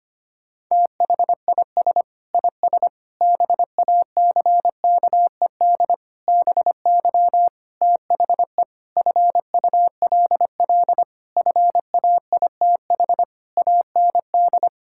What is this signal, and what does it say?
Morse code, heard as a keyed tone, about 25 wpm, 700 Hz. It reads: THIS IS BACKED BY THE FULL FAITH AND
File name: practice / 25wpm / Sentences with 9-Word Phrases - 25wpm